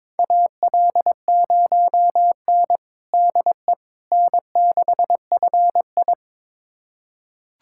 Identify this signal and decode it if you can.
Morse code, heard as a keyed tone, about 22 wpm, 700 Hz. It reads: AL0N DE N6FI
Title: USA Callsign DE Callsign - 4 Characters - 22wpm